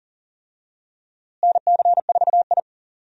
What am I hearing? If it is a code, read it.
Morse code, heard as a keyed tone, about 40 wpm, 700 Hz. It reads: NC4I
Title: USA Call Signs - 4 Characters - Repeated 2x - 40wpm